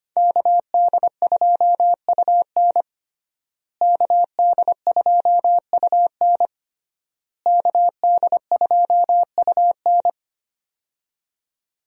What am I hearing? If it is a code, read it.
Morse code, heard as a keyed tone, about 25 wpm, 700 Hz. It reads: KD2UN KD2UN KD2UN
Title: USA Call Signs - 5 Characters - Repeated 3x - 25wpm